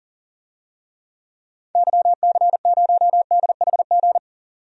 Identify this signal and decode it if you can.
Morse code, heard as a keyed tone, about 40 wpm, 700 Hz. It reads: YC0DHG
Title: DXCC-100 Call Signs - 6 Characters - 40wpm